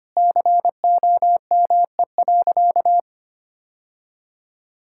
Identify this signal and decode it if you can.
Morse code, heard as a keyed tone, about 25 wpm, 700 Hz. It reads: COME.
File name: practice / 25wpm / Sentences with 2-Word Phrases - 25wpm